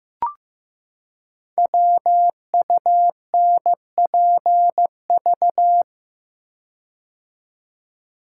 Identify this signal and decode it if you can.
Morse code, heard as a keyed tone, about 15 wpm, 700 Hz. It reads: WUNPV
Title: Code Groups - 5 Letters - 15wpm